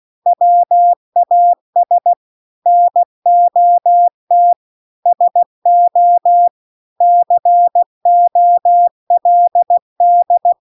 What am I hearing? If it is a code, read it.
Morse code, heard as a keyed tone, about 16 wpm, 700 Hz. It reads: WAS NOT SO COLD